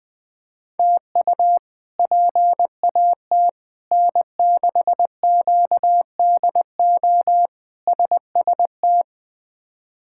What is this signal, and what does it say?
Morse code, heard as a keyed tone, about 20 wpm, 700 Hz. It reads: TU PAT N6QDO SST
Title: K1USN SST QSO Practice - 20wpm